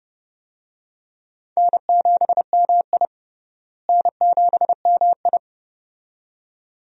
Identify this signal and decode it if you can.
Morse code, heard as a keyed tone, about 32 wpm, 700 Hz. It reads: N7MI N7MI
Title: USA Call Signs - 4 Characters - Repeated 2x - 30wpm